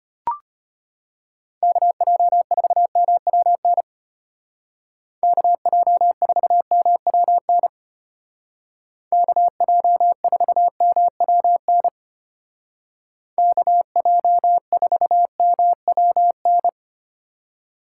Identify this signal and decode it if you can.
Morse code, heard as a keyed tone, about 31 wpm, 700 Hz. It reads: KJ4MWN KJ4MWN KJ4MWN KJ4MWN
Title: Speed Racing - USA Call Signs 25wpm